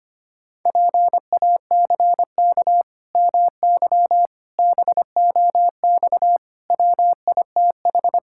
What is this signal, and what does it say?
Morse code, heard as a keyed tone, about 25 wpm, 700 Hz. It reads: PACK MY BOX WITH
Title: Pangram Challenge v2 - 25wpm